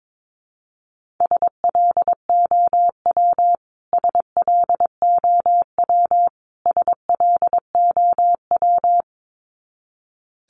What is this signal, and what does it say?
Morse code, heard as a keyed tone, about 22 wpm, 700 Hz. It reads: SLOW SLOW SLOW
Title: Top 500 Words Repeated 22wpm